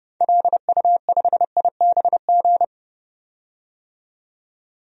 Morse code, heard as a keyed tone, about 30 wpm, 700 Hz. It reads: LU5IBG
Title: DXCC-100 Call Signs - 6 Characters - 30wpm